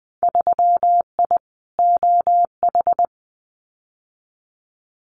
Morse code, heard as a keyed tone, about 20 wpm, 700 Hz. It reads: 3I OH
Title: Winter Field Day Exchanges Encore 20wpm